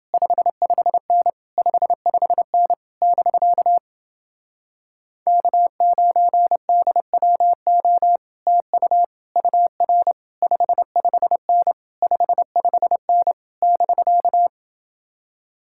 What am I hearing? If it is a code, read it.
Morse code, heard as a keyed tone, about 30 wpm, 700 Hz. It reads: 55N 55N <BK> K9DWO TU UR 55N 55N <BK>